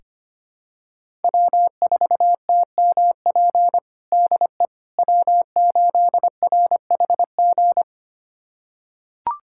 Morse code, heard as a keyed tone, about 25 wpm, 700 Hz. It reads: W4TMP DE W8RHG
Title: USA Callsign DE Callsign - 5 Characters - 25wpm